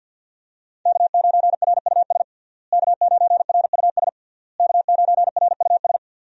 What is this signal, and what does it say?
Morse code, heard as a keyed tone, about 51 wpm, 700 Hz. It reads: K9RUS K9RUS K9RUS
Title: USA Call Signs - 5 Characters - Repeated 3x - 50wpm